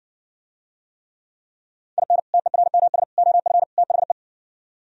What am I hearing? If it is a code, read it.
Morse code, heard as a keyed tone, about 70 wpm, 700 Hz. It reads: IN TERMS OF THE